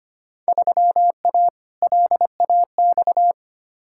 Morse code, heard as a keyed tone, about 25 wpm, 700 Hz. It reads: 3A LAX
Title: Field Day Exchanges - 25wpm